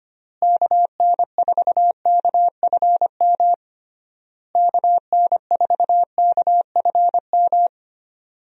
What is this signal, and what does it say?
Morse code, heard as a keyed tone, about 25 wpm, 700 Hz. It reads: KN4KFM KN4KFM
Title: USA Call Signs - 6 Characters - Repeated 3x - 25wpm